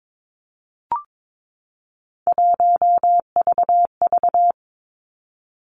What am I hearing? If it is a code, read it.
Morse code, heard as a keyed tone, about 22 wpm, 700 Hz. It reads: 1VV
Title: Three Challenging Letter-Number Combinations 22wpm